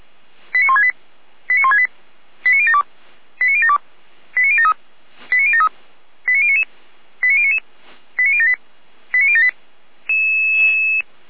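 Five-tone selective call (ZVEI), heard as quick runs of five tones, tones of about 70 ms. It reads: 89168 89168 89072 89072 89073 89073 89099 89099 89078 89078
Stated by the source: ZVEI five-tone selective call received on SDR.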